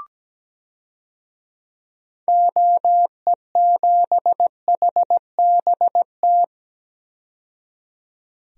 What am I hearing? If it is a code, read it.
Morse code, heard as a keyed tone, about 17 wpm, 700 Hz. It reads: OE7HBT
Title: DXCC-100 Call Signs - 6 Characters - 17wpm